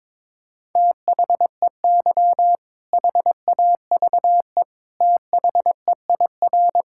Morse code, heard as a keyed tone, about 22 wpm, 700 Hz. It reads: THEY HAVE THEIR
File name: practice / 22wpm / Sentences with 2-Word Phrases - 22wpm